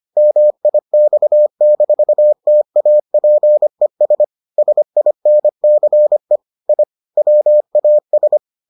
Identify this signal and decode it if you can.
Morse code, heard as a keyed tone, about 25 wpm, 600 Hz. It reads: MIX-TAPES SINCE I WAS